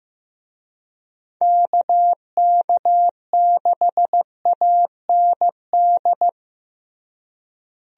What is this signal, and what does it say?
Morse code, heard as a keyed tone, about 15 wpm, 700 Hz. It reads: KK6AND
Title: USA Call Signs - 6 Characters - Repeated 2x - 15wpm